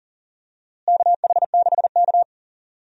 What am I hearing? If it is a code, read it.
Morse code, heard as a keyed tone, about 37 wpm, 700 Hz. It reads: KH6K
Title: DXCC-100 Call Signs - 4 Characters - Repeated 3x - 40wpm